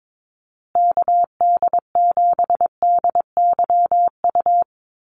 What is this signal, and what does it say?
Morse code, heard as a keyed tone, about 22 wpm, 700 Hz. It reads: KD7DYU
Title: USA Call Signs Repeated 3x 22wpm